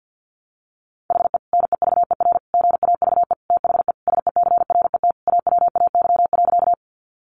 Morse code, heard as a keyed tone, about 50 wpm, 700 Hz. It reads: HE NEVER GAVE THE SECRET AWAY.
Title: Sentences from Top 1200 Words 50wpm